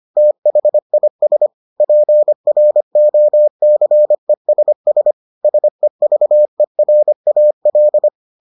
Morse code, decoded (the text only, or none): THIS PROCESS SEVERAL